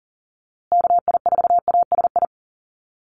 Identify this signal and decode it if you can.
Morse code, heard as a keyed tone, about 40 wpm, 700 Hz. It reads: KI4ASI
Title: USA Call Signs Repeated 3x 40wpm